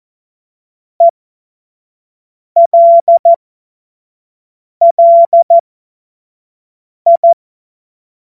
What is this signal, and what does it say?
Morse code, heard as a keyed tone, about 14 wpm, spaced as for 5 wpm, 680 Hz. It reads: ELLI
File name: Morsekurs / Witze / Witz 09 April 2024 25 F70